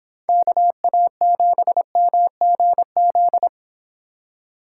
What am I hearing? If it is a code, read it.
Morse code, heard as a keyed tone, about 25 wpm, 700 Hz. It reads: KA7MGZ